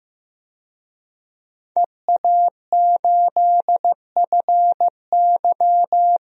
Morse code, heard as a keyed tone, about 15 wpm, 700 Hz. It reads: EA8FY
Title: DXCC-100 Call Signs - 5 Characters - 15wpm